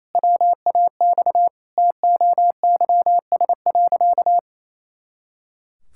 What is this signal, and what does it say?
Morse code, heard as a keyed tone, about 28 wpm, 700 Hz. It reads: WAX TOYS.
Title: Pangram Challenge v2 - 28wpm